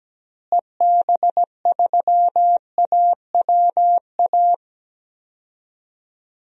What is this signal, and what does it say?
Morse code, heard as a keyed tone, about 17 wpm, 700 Hz. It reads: EB3AWA
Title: DXCC-100 Call Signs - 6 Characters - Repeated 2x - 17wpm